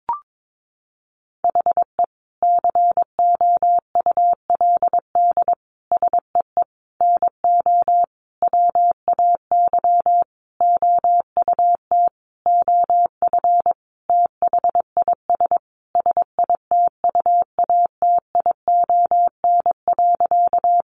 Morse code, heard as a keyed tone, about 22 wpm, 700 Hz. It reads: HE COULD SEE NO WAY OUT OF THIS SITUATION.
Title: Sentences from Top 1200 Words 22wpm